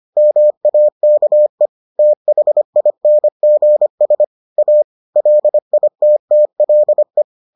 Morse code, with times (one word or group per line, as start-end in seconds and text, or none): 0.2-1.7 MAKE
2.0-4.2 THINGS
4.6-4.8 A
5.2-7.2 LITTLE